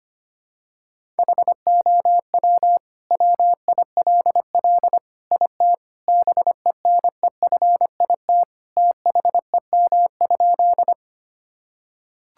Morse code, heard as a keyed tone, about 25 wpm, 700 Hz. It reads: HOW WILL IT BENEFIT THEM?
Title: Sentences from Top 1000 Words - Review Part 6 - 25wpm